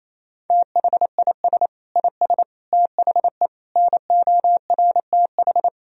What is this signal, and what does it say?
Morse code, heard as a keyed tone, about 28 wpm, 700 Hz. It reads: THIS IS THE NORTH
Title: Sentences from Top 1000 Words - Review Part 1 - 28wpm